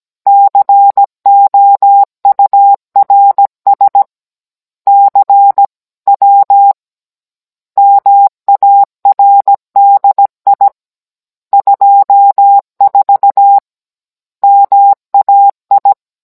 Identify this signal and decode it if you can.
Morse code, heard as a keyed tone, about 17 wpm, 800 Hz. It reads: COURS CW MARDI 24 MAI
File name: Cours-NR18-16mm